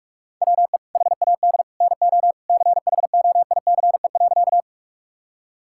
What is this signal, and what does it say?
Morse code, heard as a keyed tone, about 41 wpm, 700 Hz. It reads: WE HAD NO CHOICE.
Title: Sentences from Top 1000 Words - Review Part 6 - 45wpm